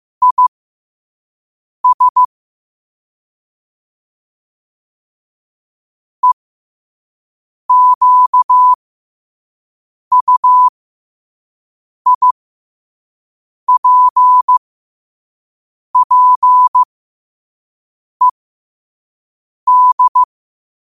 Morse code, heard as a keyed tone, about 15 wpm, spaced as for 5 wpm, 1000 Hz. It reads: IS EQUIPPED
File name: qotdfiles / 5wpm